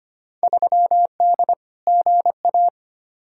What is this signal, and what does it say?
Morse code, heard as a keyed tone, about 25 wpm, 700 Hz. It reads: 3D GA
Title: Field Day Exchanges Encore - 25wpm